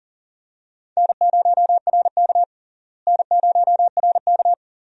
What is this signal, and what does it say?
Morse code, heard as a keyed tone, about 40 wpm, 700 Hz. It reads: N0RK N0RK
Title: USA Call Signs - 4 Characters - Repeated 2x - 40wpm